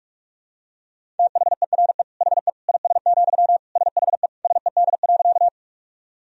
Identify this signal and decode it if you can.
Morse code, heard as a keyed tone, about 44 wpm, 700 Hz. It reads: THERE HE IS, SHE SED.
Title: Sentences from Top 100 Words and Abbreviations 45wpm